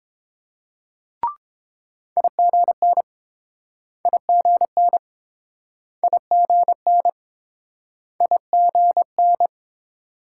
Morse code, heard as a keyed tone, about 30 wpm, 700 Hz. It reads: IGN IGN IGN IGN